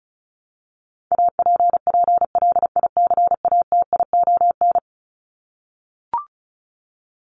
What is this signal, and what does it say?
Morse code, heard as a keyed tone, about 34 wpm, 700 Hz. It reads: APPLICATION
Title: Top 1001-1250 Words 35wpm